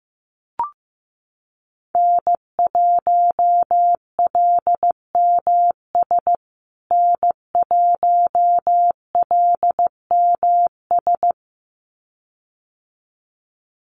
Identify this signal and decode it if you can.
Morse code, heard as a keyed tone, about 15 wpm, 700 Hz. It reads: N1LMS N1LMS
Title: USA Call Signs Repeated 2x 15wpm